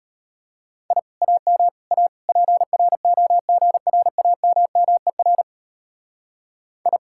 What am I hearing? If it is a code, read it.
Morse code, heard as a keyed tone, about 31 wpm, 700 Hz. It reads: I AM A PROGRAMMER I
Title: Speed-Racing Common QSO Phrases - 25wpm